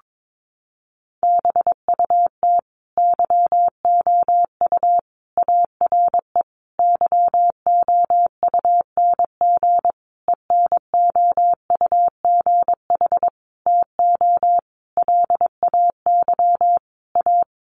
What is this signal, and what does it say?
Morse code, heard as a keyed tone, about 22 wpm, 700 Hz. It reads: BUT YOU ARE YOUNG ENOUGH TO LAY A
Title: Sentences from Top 1200 Words 22wpm